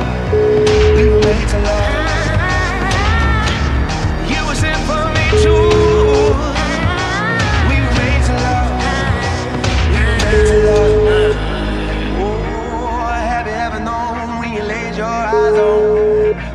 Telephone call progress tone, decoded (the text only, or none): ringback tone